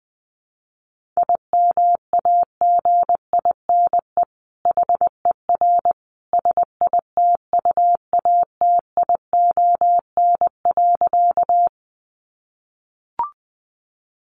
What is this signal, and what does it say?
Morse code, heard as a keyed tone, about 19 wpm, 700 Hz. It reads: IMAGINE HER SITUATION.
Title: Sentences from Top 1400 Words 20wpm